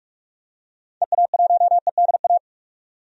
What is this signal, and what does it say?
Morse code, heard as a keyed tone, about 46 wpm, 700 Hz. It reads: EA1EDA
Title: DXCC-100 Call Signs - 6 Characters - 45wpm